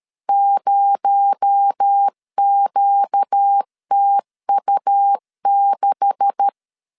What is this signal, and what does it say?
Morse code, heard as a keyed tone, about 13 wpm, 800 Hz. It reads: 0QTU6
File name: LEZIONI / 10 SIGNS / 10C DOT 12wpm 2451